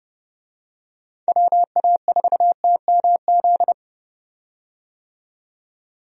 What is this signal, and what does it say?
Morse code, heard as a keyed tone, about 32 wpm, 700 Hz. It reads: WA4TMZ
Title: USA Call Signs - 6 Characters - 30wpm